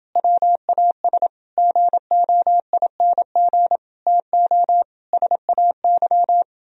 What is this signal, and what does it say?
Morse code, heard as a keyed tone, about 29 wpm, 700 Hz. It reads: WAS GOING TO SAY